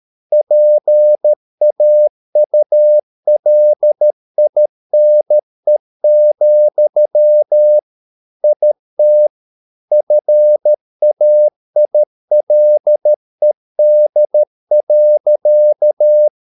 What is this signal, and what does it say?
Morse code, heard as a keyed tone, about 13 wpm, 600 Hz. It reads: PAULINE, IT FAILED.